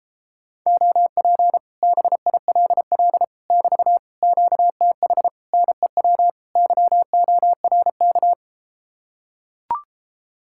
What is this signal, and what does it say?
Morse code, heard as a keyed tone, about 33 wpm, 700 Hz. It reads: OP BILL <BT> QTH NEW YORK